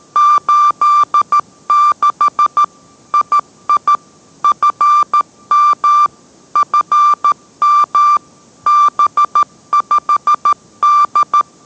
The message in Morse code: 86 II FM FM B5D

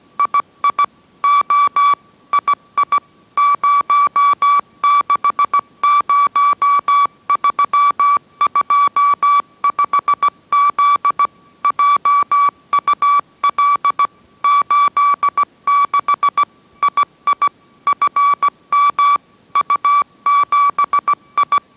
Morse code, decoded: II O II 060325Z JUL 86 II FM U7I

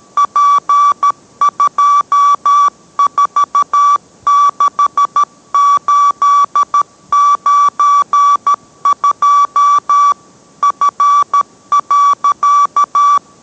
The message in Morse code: P246892 F.

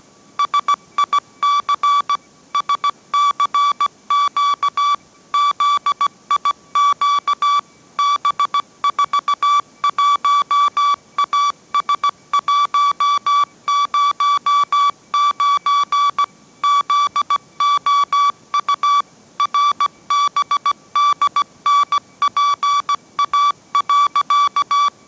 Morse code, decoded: SIC SCQ ZIQ B41AS109 ZOU RBDNPA.

